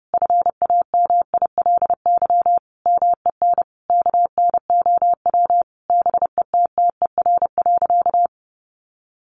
Morse code, heard as a keyed tone, about 30 wpm, 700 Hz. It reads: FAMILY MEN KNOW BETTER.
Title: Sentences from Top 1400 Words 30wpm